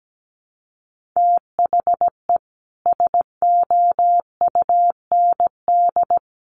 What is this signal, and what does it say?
Morse code, heard as a keyed tone, about 17 wpm, 700 Hz. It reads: THE SOUND
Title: Sentences from Top 1400 Words 17wpm